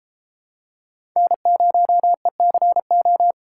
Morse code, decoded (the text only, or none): N0ECO